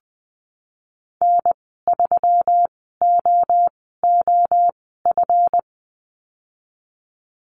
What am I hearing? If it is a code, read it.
Morse code, heard as a keyed tone, about 20 wpm, spaced as for 15 wpm, 700 Hz. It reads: N3OOF
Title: 140 Lesson 23g - Letter f - Call Signs 20wpm